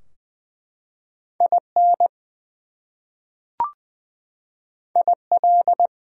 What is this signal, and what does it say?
Morse code, heard as a keyed tone, about 20 wpm, 700 Hz. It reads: IN IL
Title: US States and Canadian Provinces 20wpm